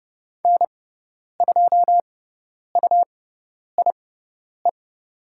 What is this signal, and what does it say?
Morse code, heard as a keyed tone, about 30 wpm, spaced as for 10 wpm, 700 Hz. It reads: N2UIE
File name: Lesson 18g - Letter m - Call Signs 30wpm